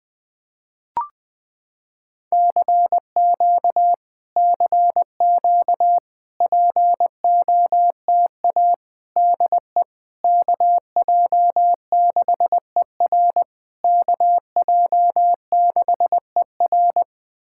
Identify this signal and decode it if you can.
Morse code, heard as a keyed tone, about 21 wpm, 700 Hz. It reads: CQ CQ POTA DE KJ6ER KJ6ER
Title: POTA QSO Practice - 20wpm